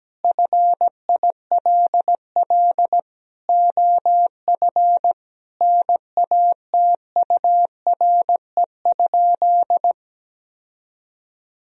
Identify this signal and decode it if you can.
Morse code, heard as a keyed tone, about 17 wpm, 700 Hz. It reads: FILL OF NATURE?
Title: Sentences from Top 1000 Words - Review Part 3 - 17wpm